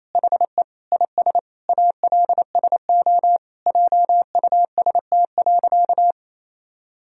Morse code, heard as a keyed tone, about 27 wpm, 700 Hz. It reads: HE IS ALSO JUST.
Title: Sentences from Top 1000 Words - Review Part 1 - 28wpm